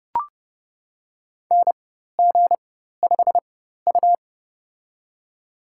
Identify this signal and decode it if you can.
Morse code, heard as a keyed tone, about 30 wpm, spaced as for 14 wpm, 690 Hz. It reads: NG5U
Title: USA Call Signs - 4x Character Spacing - 30wpm